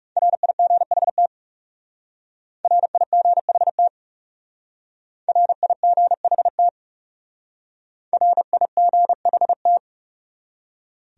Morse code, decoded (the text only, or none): RIGHT RIGHT RIGHT RIGHT